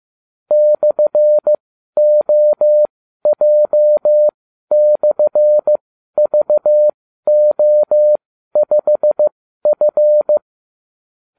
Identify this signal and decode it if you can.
Morse code, heard as a keyed tone, about 15 wpm, spaced as for 12 wpm, 600 Hz. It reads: /OJ/VO5F